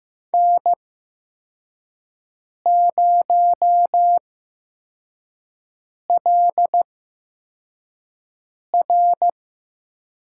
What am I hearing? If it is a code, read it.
Morse code, heard as a keyed tone, about 15 wpm, spaced as for 4 wpm, 700 Hz. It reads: N0LR
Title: USA Call Signs - 5 Characters - 8x Character Spacing - 15wpm